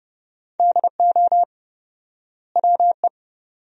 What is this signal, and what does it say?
Morse code, heard as a keyed tone, about 30 wpm, 700 Hz. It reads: DO WE
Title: 4x Word-Spacing - Sentences from Top 300 Words - 30wpm